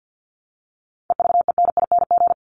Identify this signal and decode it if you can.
Morse code, heard as a keyed tone, about 50 wpm, 700 Hz. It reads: EVENING
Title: Top 1501-1750 Words 50wpm